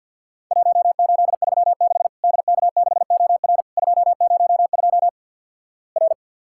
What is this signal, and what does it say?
Morse code, heard as a keyed tone, about 50 wpm, 700 Hz. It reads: 1836 DK6OR 202 R
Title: Worked All Europe DX Contest - QTC Exchange with Repeats - 50wpm